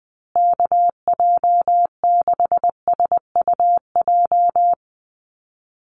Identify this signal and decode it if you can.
Morse code, heard as a keyed tone, about 20 wpm, 700 Hz. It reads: KJ6SUJ
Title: USA Call Signs Repeated 2x 20wpm